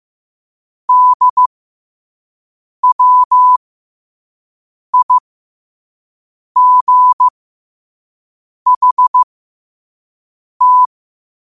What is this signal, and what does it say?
Morse code, heard as a keyed tone, about 15 wpm, spaced as for 5 wpm, 1000 Hz. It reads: DWIGHT